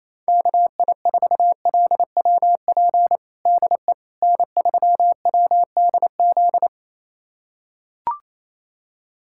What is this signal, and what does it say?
Morse code, heard as a keyed tone, about 27 wpm, 700 Hz. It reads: KI4LWP DE N3WDZ
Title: USA Callsign DE Callsign 28wpm